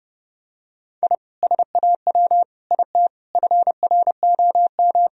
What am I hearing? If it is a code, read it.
Morse code, heard as a keyed tone, about 30 wpm, 700 Hz. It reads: I SAW IT FROM
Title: Sentences from Top 1000 Words - Review Part 2 - 30wpm